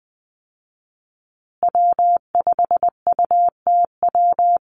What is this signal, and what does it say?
Morse code, heard as a keyed tone, about 20 wpm, 700 Hz. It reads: W5UTW
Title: Call Signs Repeated 3x - 20wpm